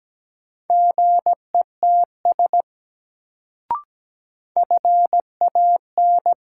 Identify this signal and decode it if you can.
Morse code, heard as a keyed tone, about 17 wpm, 700 Hz. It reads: GETS FAN
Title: Sentences from Top 1300 Words - New Word Drill Alt - 17wpm